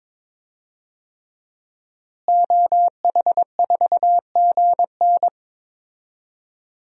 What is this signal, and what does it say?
Morse code, heard as a keyed tone, about 22 wpm, 700 Hz. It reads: OH4GN
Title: DXCC-100 Call Signs - 5 Characters - Repeated 3x - 22wpm